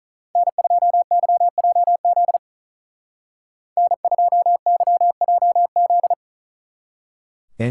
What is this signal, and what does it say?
Morse code, heard as a keyed tone, about 44 wpm, 700 Hz. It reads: N2YJZ N2YJZ
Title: Speed Racing - USA 1x3 Call Signs 35wpm